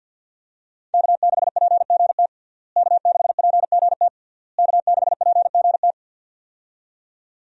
Morse code, heard as a keyed tone, about 50 wpm, 700 Hz. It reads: K6PGT K6PGT K6PGT
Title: USA Call Signs - 5 Characters - Repeated 3x - 50wpm